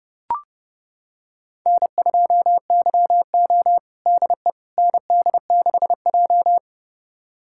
Morse code, heard as a keyed tone, about 30 wpm, 700 Hz. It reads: N2YO DE ND6J
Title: USA Callsign DE Callsign - 4 Characters - 30wpm